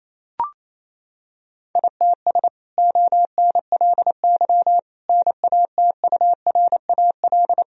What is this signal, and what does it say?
Morse code, heard as a keyed tone, about 28 wpm, 700 Hz. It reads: ITS ONLY NATURAL
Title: Sentences from Top 1000 Words - Review Part 4 - 28wpm